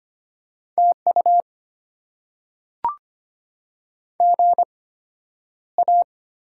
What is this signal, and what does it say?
Morse code, heard as a keyed tone, about 25 wpm, spaced as for 11 wpm, 700 Hz. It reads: TU GA
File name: Top 90 QSO Elements - 8x Character Spacing - 25wpm